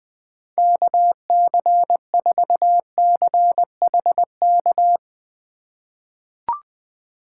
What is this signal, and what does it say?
Morse code, heard as a keyed tone, about 20 wpm, 700 Hz. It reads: KC4CHK
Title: USA Call Signs - 6 Characters - 2x Character Spacing - 20wpm